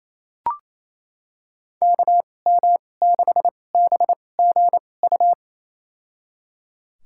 Morse code, heard as a keyed tone, about 28 wpm, spaced as for 20 wpm, 700 Hz. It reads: KM6BGU
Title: USA Call Signs - 6 Characters - 2x Character Spacing - 28wpm